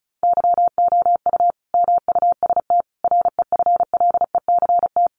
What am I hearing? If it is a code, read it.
Morse code, heard as a keyed tone, about 34 wpm, 700 Hz. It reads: YOU MUST REFLECT
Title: Sentences from Top 1500 Words 35wpm